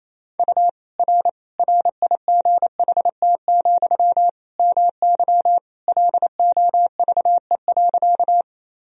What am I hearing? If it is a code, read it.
Morse code, heard as a keyed tone, about 28 wpm, 700 Hz. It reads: U R RIGHT, MY LOVE.
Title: Sentences from Top 400 Words and Abbreviations 28wpm